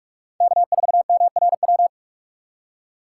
Morse code, heard as a keyed tone, about 44 wpm, 700 Hz. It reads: K4MRW